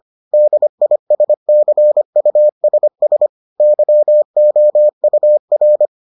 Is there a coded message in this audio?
Yes: Morse code from 0.3 to 5.9 s, heard as a keyed tone, about 25 wpm, 600 Hz. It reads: DISCUSS YOUR